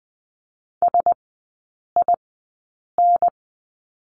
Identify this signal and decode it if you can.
Morse code, heard as a keyed tone, about 20 wpm, spaced as for 8 wpm, 700 Hz. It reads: SIN
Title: 053 Lesson 10e - Letter r - Three Characters 20wpm